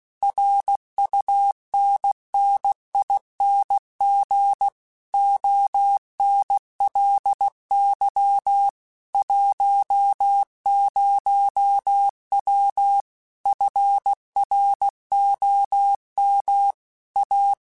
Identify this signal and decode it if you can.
Morse code, heard as a keyed tone, about 16 wpm, 800 Hz. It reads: RUNNING ONLY 10W FROM A
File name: QSOs / QSOs-mp3s / kc7sac@15